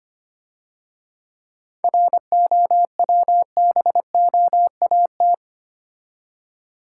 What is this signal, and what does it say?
Morse code, heard as a keyed tone, about 30 wpm, 700 Hz. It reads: ROWBOAT
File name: Easy Words - Speed-Racing with Hints - 25wpm